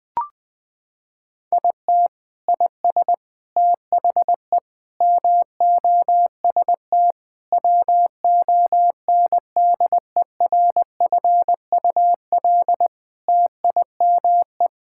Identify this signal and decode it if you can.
Morse code, heard as a keyed tone, about 20 wpm, 700 Hz. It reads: IT IS THE MOST WONDERFUL TIME
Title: Sentences with 8-Word Phrases - 20wpm